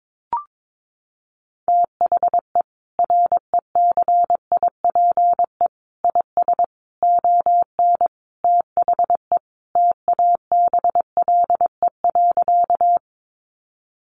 Morse code, heard as a keyed tone, about 22 wpm, 700 Hz. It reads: THE RECIPE IS ON THE TABLE.
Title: Sentences from Top 1200 Words 22wpm